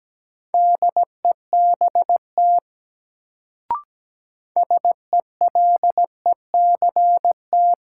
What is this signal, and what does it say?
Morse code, heard as a keyed tone, about 17 wpm, 700 Hz. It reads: DEBT SELECT
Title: Sentences from Top 1300 Words - New Word Drill - 17wpm